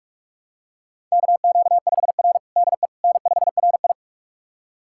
Morse code, heard as a keyed tone, about 46 wpm, 700 Hz. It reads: KQ5R DE N5RI
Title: USA Callsign DE Callsign - 4 Characters - 45wpm